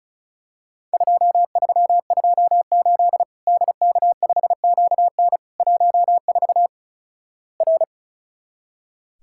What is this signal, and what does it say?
Morse code, heard as a keyed tone, about 36 wpm, 700 Hz. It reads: 2328 DK5QN 14 R
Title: Worked All Europe DX Contest - QTC Exchange - 35wpm